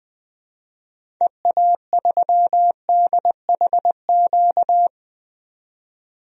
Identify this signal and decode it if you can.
Morse code, heard as a keyed tone, about 20 wpm, 700 Hz. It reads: EA3DHQ
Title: DXCC-100 Call Signs - 6 Characters - 20wpm